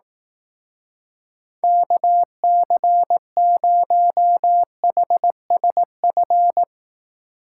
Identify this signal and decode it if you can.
Morse code, heard as a keyed tone, about 18 wpm, 700 Hz. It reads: KC0HSF
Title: Speed Racing - USA Call Signs 15wpm